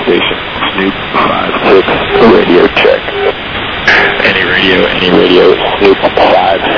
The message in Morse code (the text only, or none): none